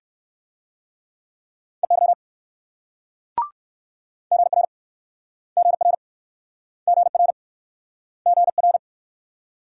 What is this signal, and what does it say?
Morse code, heard as a keyed tone, about 64 wpm, 690 Hz. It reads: EY GR GR GR GR